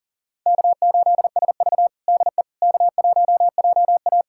Morse code, heard as a keyed tone, about 38 wpm, 700 Hz. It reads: K8SV DE K1JA
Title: USA Callsign DE Callsign - 4 Characters - 40wpm